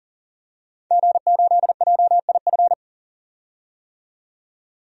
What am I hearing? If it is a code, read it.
Morse code, heard as a keyed tone, about 40 wpm, 700 Hz. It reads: G8JIF
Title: DXCC-100 Call Signs - 5 Characters - 40wpm